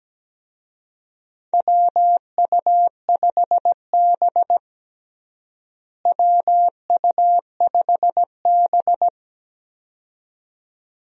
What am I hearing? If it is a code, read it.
Morse code, heard as a keyed tone, about 17 wpm, 700 Hz. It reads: WU5B WU5B
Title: USA Call Signs - 4 Characters - Repeated 2x - 17wpm